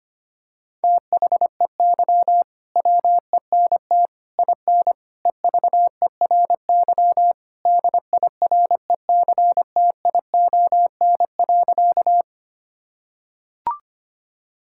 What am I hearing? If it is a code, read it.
Morse code, heard as a keyed tone, about 25 wpm, 700 Hz. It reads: THEY WENT IN EVERY DIRECTION.
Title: Sentences from Top 1000 Words - Review Part 5 - 25wpm